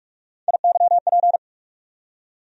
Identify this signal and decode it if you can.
Morse code, heard as a keyed tone, about 45 wpm, 700 Hz. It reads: IYP
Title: USA Call Sign Postfixes - 3 Letters 45wpm